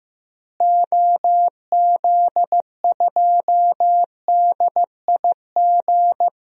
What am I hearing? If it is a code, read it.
Morse code, heard as a keyed tone, about 15 wpm, 700 Hz. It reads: OZ2DIG
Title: DXCC-100 Call Signs - 6 Characters - Repeated 2x - 15wpm